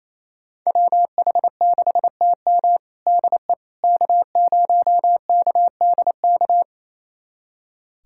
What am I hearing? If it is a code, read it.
Morse code, heard as a keyed tone, about 28 wpm, 700 Hz. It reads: WH6TM DE K0KDK
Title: USA Callsign DE Callsign - 5 Characters - 28wpm